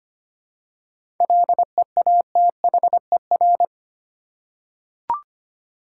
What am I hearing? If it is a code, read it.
Morse code, heard as a keyed tone, about 25 wpm, 700 Hz. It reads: LEATHER
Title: Seven Letter Words - 25wpm